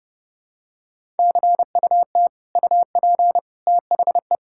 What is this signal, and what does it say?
Morse code, heard as a keyed tone, about 32 wpm, 700 Hz. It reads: CUT UP THE
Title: Sentences from Top 1000 Words - Review Part 5 - 30wpm